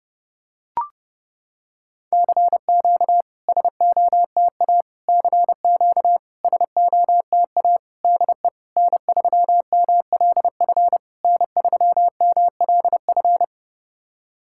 Morse code, heard as a keyed tone, about 32 wpm, 700 Hz. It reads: CQ SOTA CQ SOTA DE N3MLF N3MLF
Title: SOTA QSO Practice 30wpm